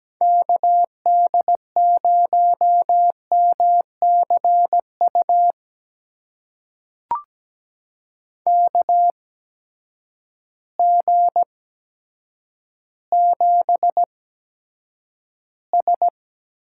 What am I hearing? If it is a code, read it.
Morse code, heard as a keyed tone, about 17 wpm, spaced as for 8 wpm, 700 Hz. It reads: KD0MCU KG7S
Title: USA Call Signs - 6 Characters - 8x Character Spacing - 17wpm